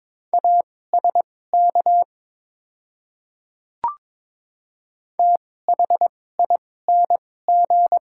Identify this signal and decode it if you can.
Morse code, heard as a keyed tone, about 22 wpm, spaced as for 16 wpm, 690 Hz. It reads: ASK THING
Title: 2x Character Spacing - Top 200 Words - 22wpm